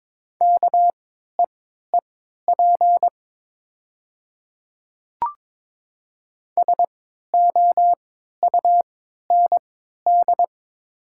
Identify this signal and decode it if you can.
Morse code, heard as a keyed tone, about 22 wpm, spaced as for 13 wpm, 700 Hz. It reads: KEEP SOUND
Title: 3x Character Spacing - Top 300 Words - 22wpm